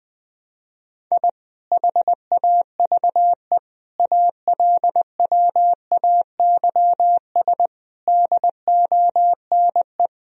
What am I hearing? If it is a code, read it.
Morse code, heard as a keyed tone, about 20 wpm, 700 Hz. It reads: I HAVE ALWAYS DONE
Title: Sentences from Top 1000 Words - Review Part 2 - 20wpm